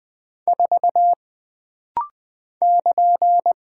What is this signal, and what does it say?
Morse code, heard as a keyed tone, about 20 wpm, 700 Hz. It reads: 4 <KN>
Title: Single Character (Letter:Number:Punctuation:Prosign) - Rapid-Fire - 20wpm